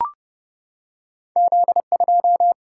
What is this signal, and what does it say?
Morse code, heard as a keyed tone, about 32 wpm, 700 Hz. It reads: Z 2
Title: Single-Letter Single-Number 30wpm